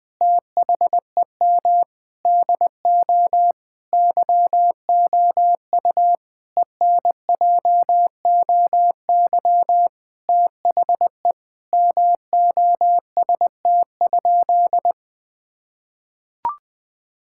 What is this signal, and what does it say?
Morse code, heard as a keyed tone, about 20 wpm, 700 Hz. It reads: THEM DO YOU ENJOY THE MOST?
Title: Sentences from Top 1000 Words - Review Part 5 - 20wpm